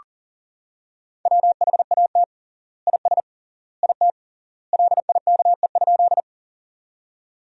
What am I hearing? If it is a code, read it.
Morse code, heard as a keyed tone, about 37 wpm, 700 Hz. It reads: WHAT IS IT LIKE?
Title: 3x Word-Spacing - Sentences from Top 100 Words - 40wpm